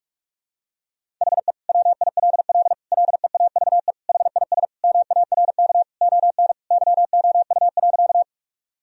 Morse code, heard as a keyed tone, about 46 wpm, 700 Hz. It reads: HE WILL LEAVE HIS MARK ON YOU.
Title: Sentences from Top 1000 Words - Review Part 1 - 45wpm